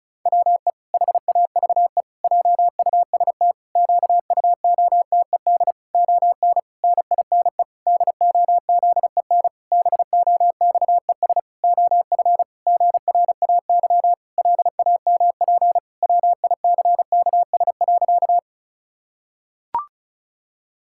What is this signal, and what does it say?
Morse code, heard as a keyed tone, about 34 wpm, 700 Hz. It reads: WE HAVE JUST QUOTED ON NINE DOZEN BOXES OF GRAY LAMP WICKS.
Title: Pangram Challenge v2 - 35wpm